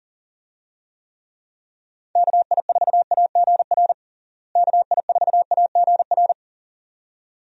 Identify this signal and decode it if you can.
Morse code, heard as a keyed tone, about 40 wpm, 700 Hz. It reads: KI4AGR KI4AGR
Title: USA Call Signs - 6 Characters - Repeated 2x - 40wpm